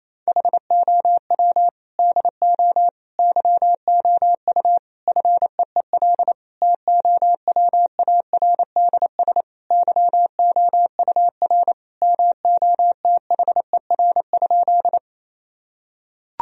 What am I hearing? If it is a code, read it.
Morse code, heard as a keyed tone, about 28 wpm, 700 Hz. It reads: HOW DO YOU FEEL TOWARDS YOUR MOTHER?
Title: Sentences from Top 1000 Words - Review Part 2 - 28wpm